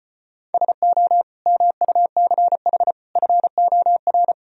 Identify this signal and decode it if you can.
Morse code, heard as a keyed tone, about 33 wpm, 700 Hz. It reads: SO MUCH FOR